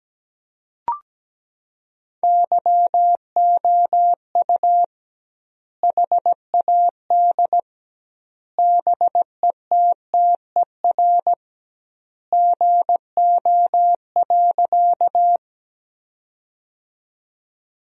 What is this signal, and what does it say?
Morse code, heard as a keyed tone, about 17 wpm, 700 Hz. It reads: YOU HAD BETTER GO.
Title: 2x Word-Spacing - Sentences from Top 400 Words - 17wpm